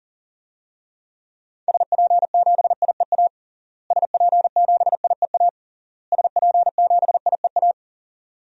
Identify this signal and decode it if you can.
Morse code, heard as a keyed tone, about 40 wpm, 700 Hz. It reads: SP7IEA SP7IEA SP7IEA
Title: DXCC-100 Call Signs - 6 Characters - Repeated 3x - 40wpm